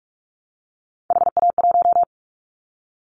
Morse code, heard as a keyed tone, about 46 wpm, 700 Hz. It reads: HA1
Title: Callsign Prefixes for Top 25 DXCC - 45wpm